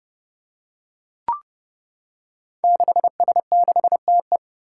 Morse code, heard as a keyed tone, about 30 wpm, 700 Hz. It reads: 6S6TE
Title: Five Challenging Letter-Number Combinations 30wpm